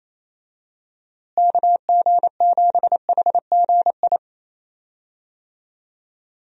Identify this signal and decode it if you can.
Morse code, heard as a keyed tone, about 28 wpm, 700 Hz. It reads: KG7HGI
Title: USA Call Signs - 6 Characters - Repeated 2x - 28wpm